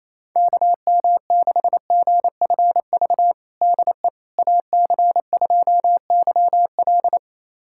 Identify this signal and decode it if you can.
Morse code, heard as a keyed tone, about 28 wpm, 700 Hz. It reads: KM6GFV DE AC2YL
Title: USA Callsign DE Callsign 28wpm